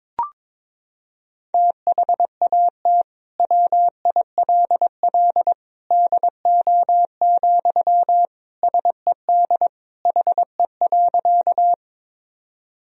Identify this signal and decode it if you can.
Morse code, heard as a keyed tone, about 22 wpm, 700 Hz. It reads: THAT WILL DO, SED HE.
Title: Sentences from Top 100 Words and Abbreviations 22wpm